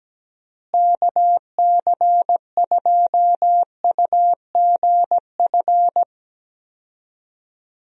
Morse code, heard as a keyed tone, about 17 wpm, 700 Hz. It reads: KC2UGF
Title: USA Call Signs - 6 Characters - Repeated 2x - 17wpm